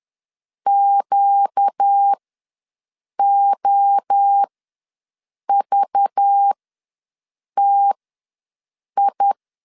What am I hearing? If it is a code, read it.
Morse code, heard as a keyed tone, about 10 wpm, spaced as for 6 wpm, 800 Hz. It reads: QOVTI
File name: LEZIONI / 7 GF9Z / 7 GF9Z ws 10wpm 3919